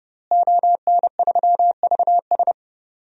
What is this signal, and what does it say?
Morse code, heard as a keyed tone, about 30 wpm, 700 Hz. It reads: ON3VS
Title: DXCC-100 Call Signs - 5 Characters - Repeated 3x - 30wpm